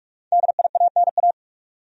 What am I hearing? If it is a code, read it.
Morse code, heard as a keyed tone, about 45 wpm, 700 Hz. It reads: DIANA
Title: USA Common Names - 45wpm